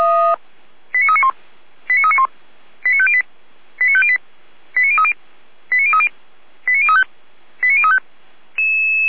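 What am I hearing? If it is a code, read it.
Five-tone selective call (ZVEI), heard as quick runs of five tones, tones of about 70 ms. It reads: 89391 89391 89508 89508 89030 89030 89035 89035
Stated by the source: ZVEI five-tone selective call received on SDR.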